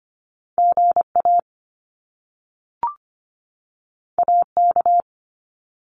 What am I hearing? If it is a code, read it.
Morse code, heard as a keyed tone, about 25 wpm, 700 Hz. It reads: GA AK
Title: US States 25wpm